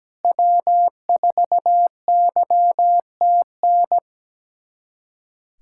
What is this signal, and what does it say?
Morse code, heard as a keyed tone, about 17 wpm, 700 Hz. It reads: W4YTN
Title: USA Callsign DE Callsign - 5 Characters - 17wpm